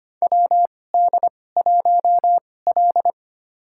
Morse code, heard as a keyed tone, about 25 wpm, spaced as for 18 wpm, 700 Hz. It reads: WD1L
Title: USA Call Signs - 4 Characters - 2x Character Spacing - 25wpm